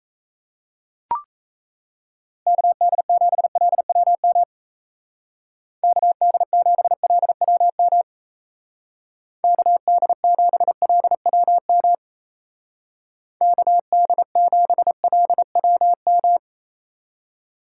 Morse code, heard as a keyed tone, about 33 wpm, 700 Hz. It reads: KD7LWM KD7LWM KD7LWM KD7LWM
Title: Speed Racing - USA 2x3 Call Signs 28wpm